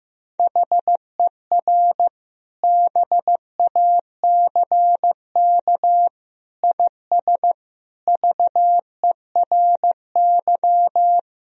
Morse code, heard as a keyed tone, about 15 wpm, 700 Hz. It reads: HER BACK IS VERY
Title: Sentences from Top 1000 Words - Review Part 1 - 15wpm